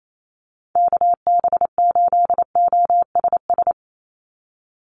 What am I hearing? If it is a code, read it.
Morse code, heard as a keyed tone, about 28 wpm, 700 Hz. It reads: KB8OSS
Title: USA Call Signs 28wpm